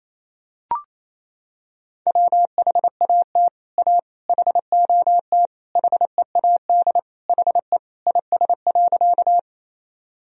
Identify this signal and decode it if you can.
Morse code, heard as a keyed tone, about 28 wpm, 700 Hz. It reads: WHAT A HOT HEAD HE IS.
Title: Sentences from Top 1000 Words - Review Part 3 - 28wpm